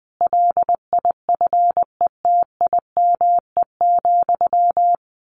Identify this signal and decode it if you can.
Morse code, heard as a keyed tone, about 20 wpm, 700 Hz. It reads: LIFETIME,
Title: A Sound Of Thunder 20wpm